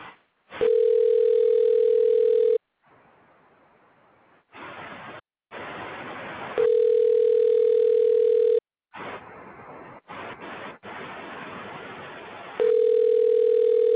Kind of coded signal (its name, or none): telephone call progress tone